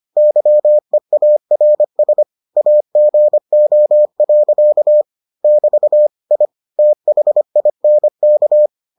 Morse code, heard as a keyed tone, about 25 wpm, 600 Hz. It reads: YEARS AGO. <BT> I THINK